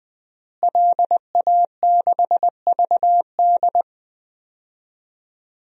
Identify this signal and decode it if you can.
Morse code, heard as a keyed tone, about 21 wpm, 700 Hz. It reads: LA6VD
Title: DXCC-100 Call Signs - 5 Characters - Repeated 3x - 20wpm